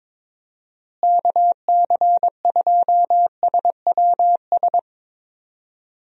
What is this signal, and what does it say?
Morse code, heard as a keyed tone, about 22 wpm, 700 Hz. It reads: KC2SWS
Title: USA Call Signs - 6 Characters - Repeated 3x - 22wpm